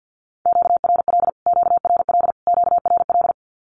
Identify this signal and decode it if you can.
Morse code, heard as a keyed tone, about 48 wpm, 700 Hz. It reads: QRL QRL QRL
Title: Q-Codes Repeated 3x 50wpm